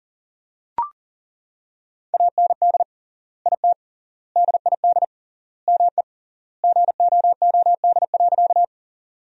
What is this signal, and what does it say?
Morse code, heard as a keyed tone, about 40 wpm, 700 Hz. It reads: AND IT DID ME GOOD.
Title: 3x Word-Spacing - Sentences from Top 200 Words - 40wpm